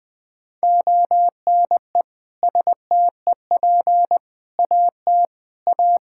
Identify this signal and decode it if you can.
Morse code, heard as a keyed tone, about 20 wpm, 700 Hz. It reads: ONE STEP AT A
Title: Sentences with 3-Word Phrases - 20wpm